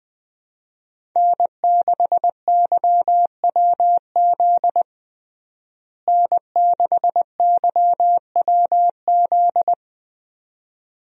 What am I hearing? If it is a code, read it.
Morse code, heard as a keyed tone, about 21 wpm, 700 Hz. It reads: N6YWZ N6YWZ
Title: USA Call Signs - 5 Characters - Repeated 2x - 20wpm